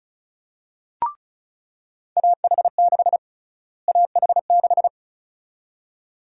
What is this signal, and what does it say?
Morse code, heard as a keyed tone, about 36 wpm, 700 Hz. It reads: AH6 AH6
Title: USA Call Sign Prefixes - Repeated 2x - 35wpm